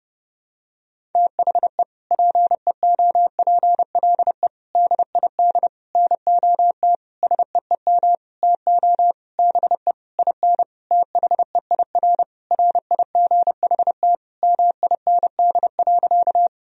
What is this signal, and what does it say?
Morse code, heard as a keyed tone, about 30 wpm, 700 Hz. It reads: THE PEOPLE DID NOT SEEM TO BE IN THEIR RIGHT MIND.
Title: Sentences from Top 1000 Words - Review Part 1 - 30wpm